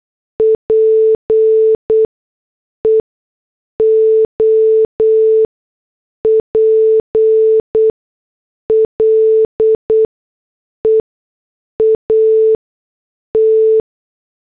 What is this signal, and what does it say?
Morse code, heard as a keyed tone, about 8 wpm, spaced as for 6 wpm, 440 Hz. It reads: PEOPLE AT